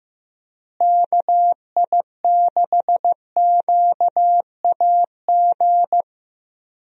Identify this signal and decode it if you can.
Morse code, heard as a keyed tone, about 15 wpm, 700 Hz. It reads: KI6QAG
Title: USA Call Signs - 6 Characters - Repeated 3x - 15wpm